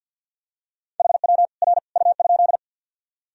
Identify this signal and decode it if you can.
Morse code, heard as a keyed tone, about 51 wpm, 700 Hz. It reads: HW R U?
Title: Sentences from Top 100 Words and Abbreviations 50wpm